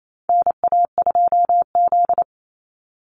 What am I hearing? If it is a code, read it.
Morse code, heard as a keyed tone, about 28 wpm, 700 Hz. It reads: NA2Z
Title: USA Call Signs 28wpm